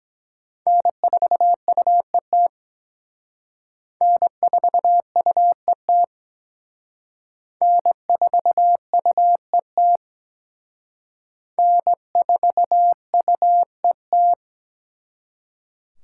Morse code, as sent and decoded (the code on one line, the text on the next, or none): -. ....- ..- . - / -. ....- ..- . - / -. ....- ..- . - / -. ....- ..- . -
N4UET N4UET N4UET N4UET